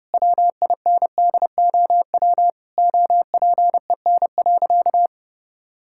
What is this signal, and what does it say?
Morse code, heard as a keyed tone, about 30 wpm, 700 Hz. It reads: WINDOW OPEN.
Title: Sentences from Top 1600 Words 30wpm